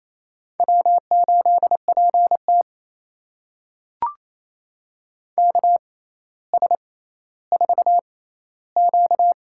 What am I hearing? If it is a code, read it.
Morse code, heard as a keyed tone, about 28 wpm, spaced as for 14 wpm, 700 Hz. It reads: W8PT KS4Q
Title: USA Call Signs - 4 Characters - 6x Character Spacing - 28wpm